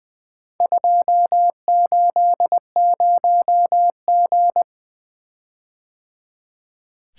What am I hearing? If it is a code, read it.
Morse code, heard as a keyed tone, about 23 wpm, 700 Hz. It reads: 280G
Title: Speed-Racing - Four Letter-Number Combinations - 20wpm